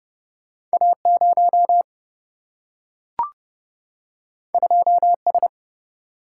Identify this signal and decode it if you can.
Morse code, heard as a keyed tone, about 29 wpm, 700 Hz. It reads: A0 2S
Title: Single-Letter Single-Number 30wpm